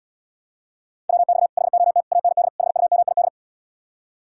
Morse code, heard as a keyed tone, about 79 wpm, 700 Hz. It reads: WX HOT AND HUMID